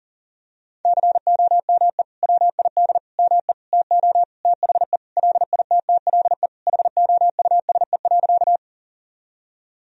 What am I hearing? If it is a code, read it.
Morse code, heard as a keyed tone, about 40 wpm, 700 Hz. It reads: COME WID ME TO THE LITTLE HOUSE.
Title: Sentences from Top 200 Words and Abbreviations 40wpm